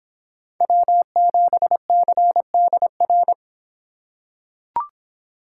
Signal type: Morse code

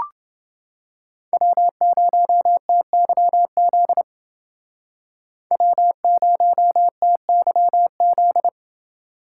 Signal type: Morse code

